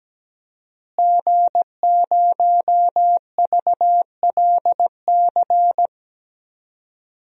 Morse code, heard as a keyed tone, about 17 wpm, 700 Hz. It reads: G0VLC
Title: DXCC-100 Call Signs - 5 Characters - Repeated 3x - 17wpm